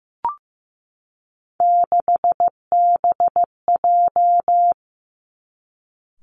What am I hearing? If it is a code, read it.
Morse code, heard as a keyed tone, about 15 wpm, 700 Hz. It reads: 6BJ
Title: Three Challenging Letter-Number Combinations 15wpm